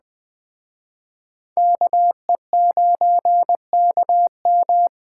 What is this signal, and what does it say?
Morse code, heard as a keyed tone, about 20 wpm, 700 Hz. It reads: KE9KM
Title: Speed Racing - USA 2x2 Call Signs 17wpm